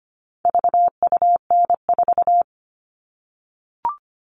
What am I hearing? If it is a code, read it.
Morse code, heard as a keyed tone, about 25 wpm, 700 Hz. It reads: VUN4
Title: All Cut Numbers - 4-Digit 25wpm